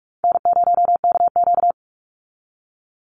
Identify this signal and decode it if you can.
Morse code, heard as a keyed tone, about 44 wpm, 700 Hz. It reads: N0KQ
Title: Easy Call Signs Repeated 2x - 45wpm